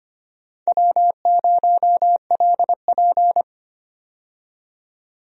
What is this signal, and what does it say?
Morse code, heard as a keyed tone, about 25 wpm, 700 Hz. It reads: W0LP
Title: USA Call Signs - 4 Characters - 25wpm